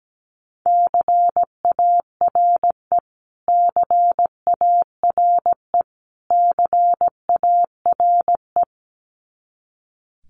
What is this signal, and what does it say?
Morse code, heard as a keyed tone, about 17 wpm, 700 Hz. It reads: CARE CARE CARE
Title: Top 300 Words Repeated 17wpm